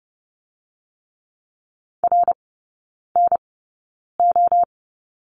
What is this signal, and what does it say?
Morse code, heard as a keyed tone, about 31 wpm, spaced as for 9 wpm, 700 Hz. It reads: RNO
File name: Lesson 16e - Letter c - Three Characters 30wpm